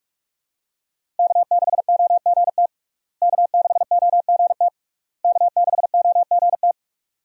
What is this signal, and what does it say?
Morse code, heard as a keyed tone, about 46 wpm, 700 Hz. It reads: K6OGT K6OGT K6OGT
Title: USA Call Signs - 5 Characters - Repeated 3x - 45wpm